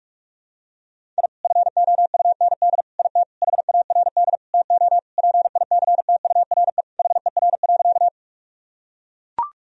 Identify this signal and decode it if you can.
Morse code, heard as a keyed tone, about 41 wpm, 700 Hz. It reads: I FOUND IT HARD TO PICTURE HER.
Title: Sentences from Top 1000 Words - Review Part 1 - 45wpm